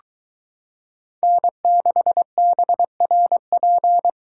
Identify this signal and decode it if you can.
Morse code, heard as a keyed tone, about 22 wpm, 700 Hz. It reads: N6BRP